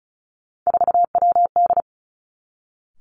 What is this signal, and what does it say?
Morse code, heard as a keyed tone, about 36 wpm, 700 Hz. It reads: 4WD
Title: Three Challenging Letter-Number Combinations 35wpm